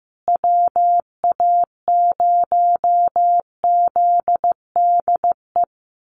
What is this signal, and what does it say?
Morse code, heard as a keyed tone, about 15 wpm, 700 Hz. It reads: WA0ZDE
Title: USA Call Signs Repeated 3x 15wpm